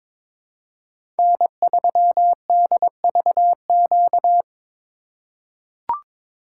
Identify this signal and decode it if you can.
Morse code, heard as a keyed tone, about 22 wpm, 700 Hz. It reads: N3DVQ
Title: USA Call Signs - 2x Character Spacing - 22wpm